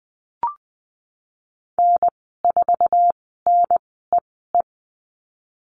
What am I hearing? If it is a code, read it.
Morse code, heard as a keyed tone, about 19 wpm, spaced as for 14 wpm, 700 Hz. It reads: N4NEE
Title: 048 Lesson 9g - Number 4 - Call Signs 20wpm